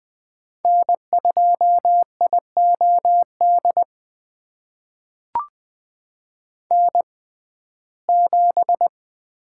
Morse code, heard as a keyed tone, about 20 wpm, spaced as for 15 wpm, 700 Hz. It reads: N2IOD N7
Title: USA Call Signs - 5 Characters - 6x Character Spacing - 20wpm